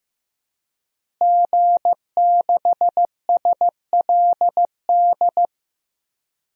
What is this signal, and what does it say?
Morse code, heard as a keyed tone, about 15 wpm, 700 Hz. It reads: G6SLD
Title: DXCC-100 Call Signs - 5 Characters - Repeated 3x - 15wpm